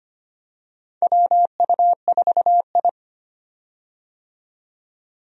Morse code, heard as a keyed tone, about 25 wpm, 700 Hz. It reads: WU4I
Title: USA Call Signs - 4 Characters - 25wpm